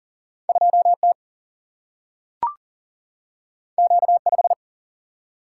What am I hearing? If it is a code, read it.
Morse code, heard as a keyed tone, about 38 wpm, 700 Hz. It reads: 2T Q5
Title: Single-Letter Single-Number 40wpm